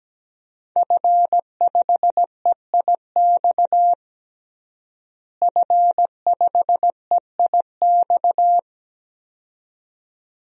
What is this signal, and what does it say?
Morse code, heard as a keyed tone, about 17 wpm, 700 Hz. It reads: F5EIX F5EIX
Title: DXCC-100 Call Signs - 5 Characters - Repeated 2x - 17wpm